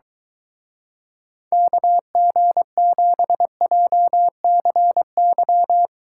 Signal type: Morse code

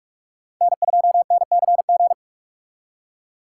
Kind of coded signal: Morse code

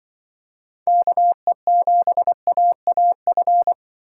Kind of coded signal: Morse code

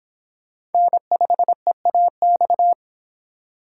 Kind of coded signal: Morse code

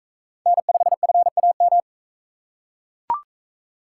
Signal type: Morse code

